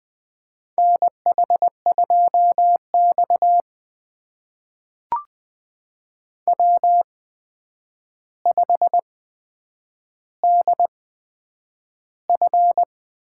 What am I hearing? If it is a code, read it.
Morse code, heard as a keyed tone, about 20 wpm, spaced as for 9 wpm, 700 Hz. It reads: NH2X W5DF
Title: USA Call Signs - 4 Characters - 8x Character Spacing - 20wpm